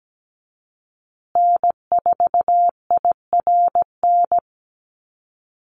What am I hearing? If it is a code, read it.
Morse code, heard as a keyed tone, about 17 wpm, 700 Hz. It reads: N4IRN
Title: Call Signs Repeated 3x - 17wpm